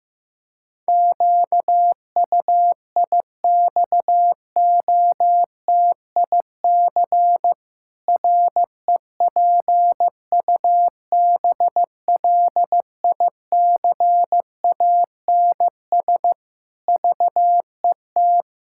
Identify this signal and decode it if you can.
Morse code, heard as a keyed tone, about 15 wpm, 700 Hz. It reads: QUIXOTIC REPUBLICANS VET
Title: Pangram Challenge v2 - 15wpm